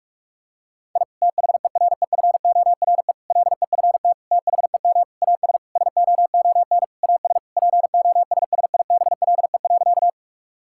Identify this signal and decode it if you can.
Morse code, heard as a keyed tone, about 41 wpm, 700 Hz. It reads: I THEREFORE LEFT THEM AS SOON AS POSSIBLE.
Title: Sentences from Top 1000 Words - Review Part 5 - 45wpm